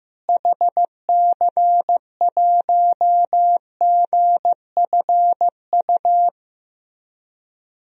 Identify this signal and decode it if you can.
Morse code, heard as a keyed tone, about 15 wpm, 700 Hz. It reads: HC1GFU
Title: DXCC-100 Call Signs - 6 Characters - Repeated 3x - 15wpm